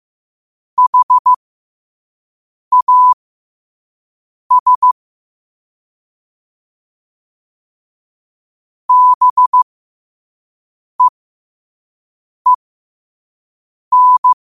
Morse code, heard as a keyed tone, about 15 wpm, spaced as for 5 wpm, 1000 Hz. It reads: HAS BEEN
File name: qotdfiles / 5wpm